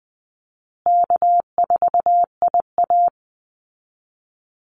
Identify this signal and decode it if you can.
Morse code, heard as a keyed tone, about 20 wpm, 700 Hz. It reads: K4IA
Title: Easy Call Signs Repeated 3x - 20wpm